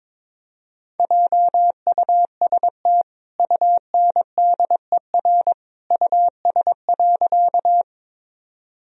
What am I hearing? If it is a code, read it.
Morse code, heard as a keyed tone, about 22 wpm, 700 Hz. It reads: JUST UNDER US.
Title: Sentences from Top 1000 Words - Review Part 1 - 22wpm